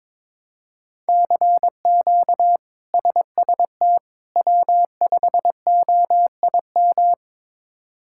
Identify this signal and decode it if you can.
Morse code, heard as a keyed tone, about 22 wpm, 700 Hz. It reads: CQ SST W5OIM
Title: K1USN SST QSO Practice - 22wpm